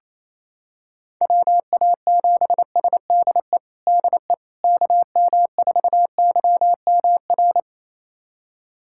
Morse code, heard as a keyed tone, about 28 wpm, 700 Hz. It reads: WA7SDE DE KM4YMR
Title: USA Callsign DE Callsign - 6 Characters - 28wpm